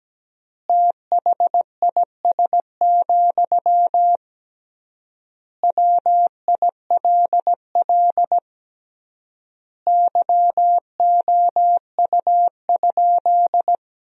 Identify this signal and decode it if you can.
Morse code, heard as a keyed tone, about 17 wpm, 700 Hz. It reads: THIS, WILL YOU?
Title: 3x Word-Spacing - Sentences from Top 400 Words - 17wpm